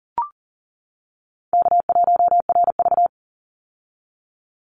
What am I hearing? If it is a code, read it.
Morse code, heard as a keyed tone, about 40 wpm, 700 Hz. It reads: K1RV
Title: Easy Call Signs - 40wpm